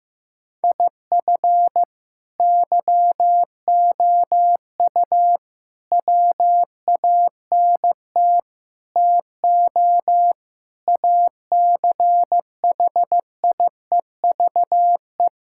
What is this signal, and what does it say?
Morse code, heard as a keyed tone, about 15 wpm, 700 Hz. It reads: IF YOU WANT TO ACHIEVE
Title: Sentences from Top 1000 Words - Review Part 6 - 15wpm